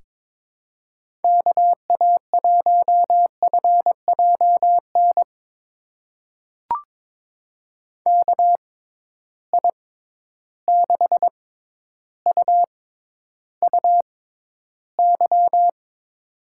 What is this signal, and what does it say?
Morse code, heard as a keyed tone, about 22 wpm, spaced as for 11 wpm, 700 Hz. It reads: KA1FJN KI6UUY
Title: USA Call Signs - 6 Characters - 6x Character Spacing - 22wpm